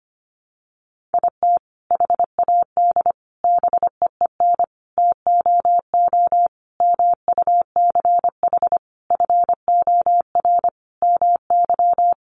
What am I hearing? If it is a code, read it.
Morse code, heard as a keyed tone, about 25 wpm, 700 Hz. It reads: IT HAD BEEN TOO MUCH FOR MY
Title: Sentences from Top 1300 Words 25wpm